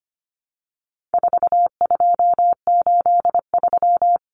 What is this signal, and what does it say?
Morse code, heard as a keyed tone, about 25 wpm, 700 Hz. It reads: 4283
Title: Common Cut Numbers - 4-Digit 25wpm